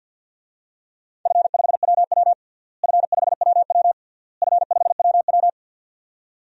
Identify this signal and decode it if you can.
Morse code, heard as a keyed tone, about 52 wpm, 700 Hz. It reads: F5WW F5WW F5WW
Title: DXCC-100 Call Signs - 4 Characters - Repeated 3x - 50wpm